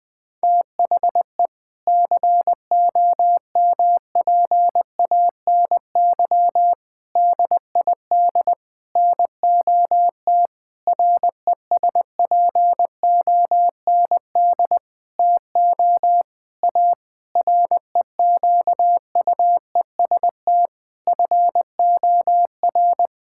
Morse code, heard as a keyed tone, about 21 wpm, 700 Hz. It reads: THE COMPANY DID NOT RESPOND TO A REQUEST FOR
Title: Sentences with 8-Word Phrases - 20wpm